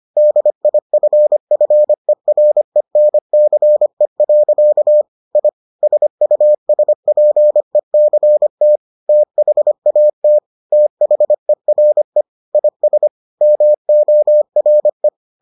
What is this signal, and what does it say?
Morse code, heard as a keyed tone, about 25 wpm, 600 Hz. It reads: DIFFERENCE. I SUSPECT THAT THERE IS MORE